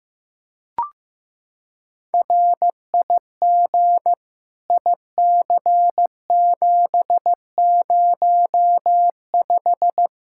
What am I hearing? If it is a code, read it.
Morse code, heard as a keyed tone, about 15 wpm, 700 Hz. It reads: RIG IC705
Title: K1USN SST QSO Practice 15wpm